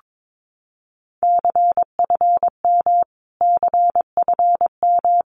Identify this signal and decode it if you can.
Morse code, heard as a keyed tone, about 22 wpm, 700 Hz. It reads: CFM CFM
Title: Abbreviations Repeated 3x - 22wpm